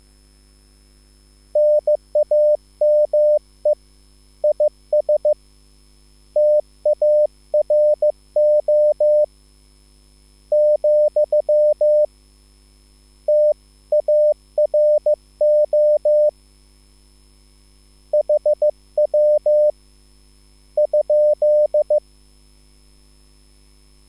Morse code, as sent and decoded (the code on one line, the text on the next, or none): -. .- -- . / .. ... / - .- .-. --- / --..-- / - .- .-. --- / .... .-- / ..--..
NAME IS TARO , TARO HW ?